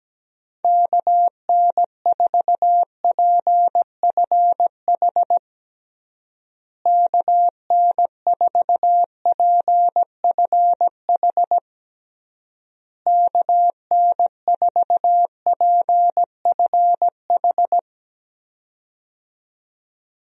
Morse code, heard as a keyed tone, about 17 wpm, 700 Hz. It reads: KN4PFH KN4PFH KN4PFH
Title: USA Call Signs - 6 Characters - Repeated 3x - 17wpm